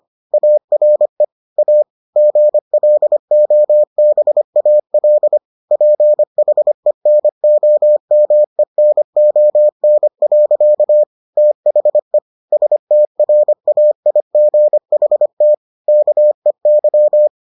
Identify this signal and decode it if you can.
Morse code, heard as a keyed tone, about 25 wpm, 600 Hz. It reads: ARE A GLOBAL PHENOMENON. THE STRAIGHT KEY